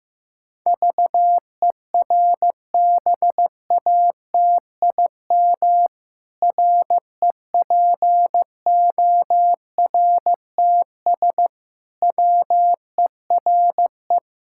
Morse code, heard as a keyed tone, about 15 wpm, 700 Hz. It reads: VERBATIM REPORTS WERE
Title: Pangram Challenge v2 - 15wpm